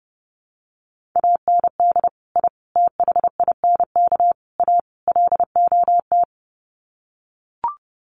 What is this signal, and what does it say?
Morse code, heard as a keyed tone, about 30 wpm, 700 Hz. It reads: AND I THINK A LOT
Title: Sets of 5 Words Encore2 30wpm